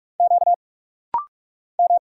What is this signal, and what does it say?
Morse code, heard as a keyed tone, about 46 wpm, 700 Hz. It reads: Q M
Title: Single Letter-Number - Rapid-Fire 45wpm